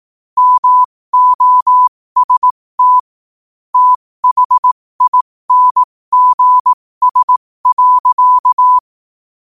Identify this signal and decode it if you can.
Morse code, heard as a keyed tone, about 18 wpm, spaced as for 15 wpm, 1000 Hz. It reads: MOST THINGS.
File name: qotdfiles / 15wpm